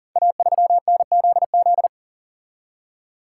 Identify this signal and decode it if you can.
Morse code, heard as a keyed tone, about 40 wpm, 700 Hz. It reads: A3NZZ
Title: Code Groups - 5 Letter:Numbers - 40wpm